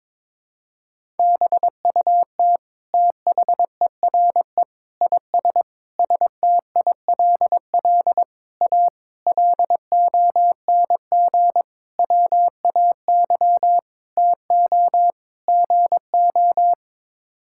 Morse code, heard as a keyed tone, about 22 wpm, 700 Hz. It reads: BUT THERE IS STILL A LONG WAY TO GO
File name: Sets of 9 Words - 22wpm